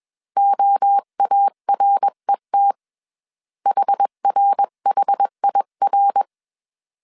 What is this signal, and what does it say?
Morse code, heard as a keyed tone, about 21 wpm, 800 Hz. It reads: OARET HRHIR
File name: 20wpm